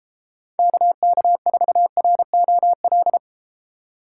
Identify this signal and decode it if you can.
Morse code, heard as a keyed tone, about 33 wpm, 700 Hz. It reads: KK4ROL